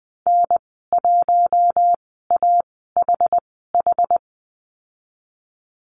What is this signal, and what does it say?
Morse code, heard as a keyed tone, about 20 wpm, spaced as for 15 wpm, 700 Hz. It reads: N1AHH
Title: 062 Lesson 11g - Letter h - Call Signs 20wpm